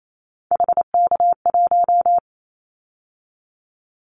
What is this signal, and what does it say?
Morse code, heard as a keyed tone, about 28 wpm, 700 Hz. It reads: HK1
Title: Callsign Prefixes for Top 100 DXCC - 28wpm